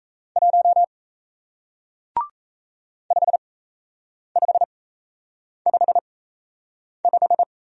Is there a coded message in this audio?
Yes: Morse code from 0.4 to 7.4 s, heard as a keyed tone, about 37 wpm, spaced as for 8 wpm, 700 Hz. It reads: 1 5555